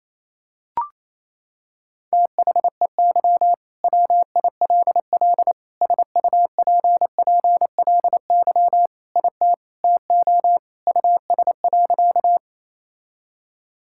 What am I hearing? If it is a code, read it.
Morse code, heard as a keyed tone, about 28 wpm, 700 Hz. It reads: THEY WILL SUPPLY IT TO US.
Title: Sentences from Top 1000 Words - Review Part 5 - 28wpm